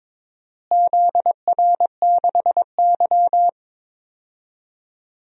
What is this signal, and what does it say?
Morse code, heard as a keyed tone, about 22 wpm, 700 Hz. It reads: ZR6Y
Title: DXCC-100 Call Signs - 4 Characters - Repeated 3x - 22wpm